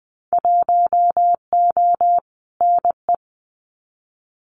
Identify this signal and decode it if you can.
Morse code, heard as a keyed tone, about 20 wpm, 700 Hz. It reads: 1O NE
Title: Winter Field Day Exchanges 20wpm